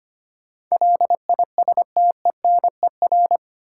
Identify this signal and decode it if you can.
Morse code, heard as a keyed tone, about 25 wpm, 700 Hz. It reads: LISTENER
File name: 8 Letter Words - First 3 Letters Given - 25wpm